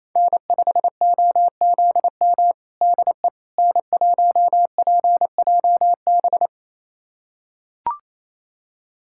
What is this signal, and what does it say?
Morse code, heard as a keyed tone, about 28 wpm, 700 Hz. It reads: N5OZM DE N1PJB
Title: USA Callsign DE Callsign - 5 Characters - 28wpm